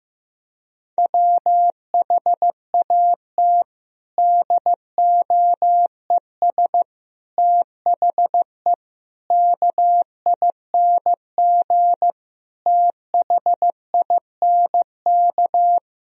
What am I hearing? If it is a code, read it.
Morse code, heard as a keyed tone, about 15 wpm, 700 Hz. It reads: WHAT DOES THE KING THINK
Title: Sentences from Top 1000 Words - Review Part 2 - 15wpm